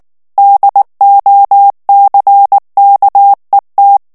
Morse code, heard as a keyed tone, about 19 wpm, 800 Hz. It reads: DOCKET
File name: ANGLAIS texte clair 3 vitesse 18 mots minute